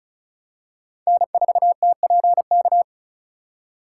Morse code, heard as a keyed tone, about 35 wpm, 690 Hz. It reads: N4TPK
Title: USA Call Signs - 5 Characters - 2x Character Spacing - 35wpm